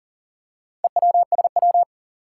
Morse code, heard as a keyed tone, about 39 wpm, 700 Hz. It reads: EWSW
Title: Four Challenging Letter-Number Combinations 40wpm